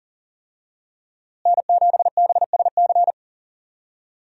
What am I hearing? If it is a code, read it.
Morse code, heard as a keyed tone, about 40 wpm, 700 Hz. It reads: N7BSC
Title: USA Call Signs - 5 Characters - Repeated 2x - 40wpm